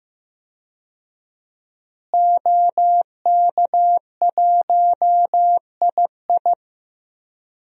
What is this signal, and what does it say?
Morse code, heard as a keyed tone, about 15 wpm, 700 Hz. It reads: OK1II
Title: DXCC-100 Call Signs - 5 Characters - Repeated 3x - 15wpm